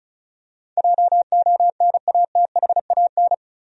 Morse code, heard as a keyed tone, about 34 wpm, 700 Hz. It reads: JONATHAN
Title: USA Common Names - 35wpm